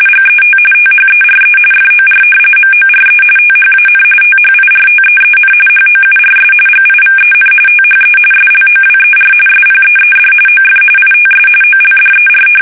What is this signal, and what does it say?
RTTY, about 45.45 baud. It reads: GET THERE FOR MORE THAN THIRTEEN HOURS GIVEN THE SLOWER SPEED OF ITS OLDER
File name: fsk850 / battleshipnj